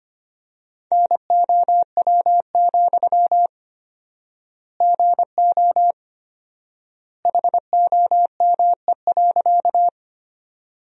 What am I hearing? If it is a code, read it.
Morse code, heard as a keyed tone, about 25 wpm, 700 Hz. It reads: NOW, GO HOME.
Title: 4x Word-Spacing - Sentences from Top 200 Words - 25wpm